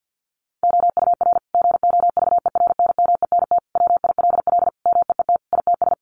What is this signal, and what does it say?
Morse code, heard as a keyed tone, about 50 wpm, 700 Hz. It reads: OUR GOVERNMENT WILL MEET ITS
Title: Sentences from Top 1200 Words 50wpm